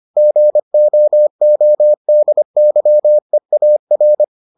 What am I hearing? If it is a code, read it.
Morse code, heard as a keyed tone, about 25 wpm, 600 Hz. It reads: GOODYEAR